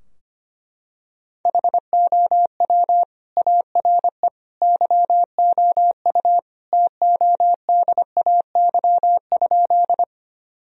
Morse code, heard as a keyed tone, about 25 wpm, 700 Hz. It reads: HOW ARE YOU TODAY?
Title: Sentences from Top 1000 Words - Review Part 4 - 25wpm